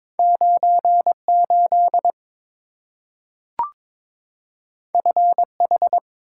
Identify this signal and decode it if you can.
Morse code, heard as a keyed tone, about 22 wpm, 690 Hz. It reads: 98 FH
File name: practice / 22wpm / Two Letter-Number Combinations - 22wpm